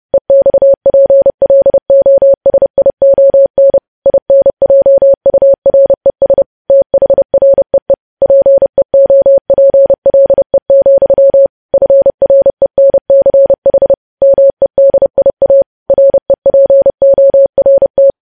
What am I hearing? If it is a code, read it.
Morse code, heard as a keyed tone, about 30 wpm, 550 Hz. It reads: EXPLOSION INJURES THREE PEOPLE, FRENCH MEDIA REPORT
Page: news-headlines-in-morse-code-at-30-wpm